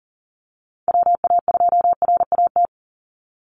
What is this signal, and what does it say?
Morse code, heard as a keyed tone, about 40 wpm, 700 Hz. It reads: WA2RAT
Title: USA Call Signs 40wpm